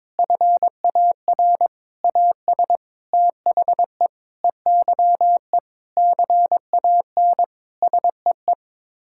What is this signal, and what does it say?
Morse code, heard as a keyed tone, about 22 wpm, 700 Hz. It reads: FAR AS THE EYE CAN SEE
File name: practice / 22wpm / Sets of 7 Words - 22wpm